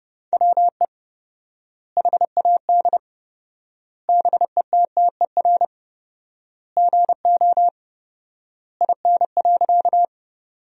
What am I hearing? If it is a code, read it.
Morse code, heard as a keyed tone, about 30 wpm, 700 Hz. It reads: WE HAD BETTER GO IN.
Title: 4x Word-Spacing - Sentences from Top 400 Words - 30wpm